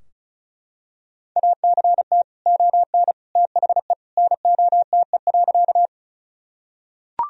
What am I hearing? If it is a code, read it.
Morse code, heard as a keyed tone, about 36 wpm, 700 Hz. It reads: ACT ON THE NOTE.
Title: Sentences from Top 1000 Words - Review Part 2 - 35wpm